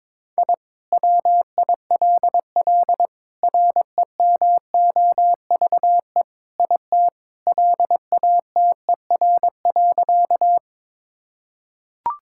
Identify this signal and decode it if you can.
Morse code, heard as a keyed tone, about 22 wpm, 700 Hz. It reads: I WILL REMOVE IT LATER.
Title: Sentences from Top 1000 Words - Review Part 5 - 22wpm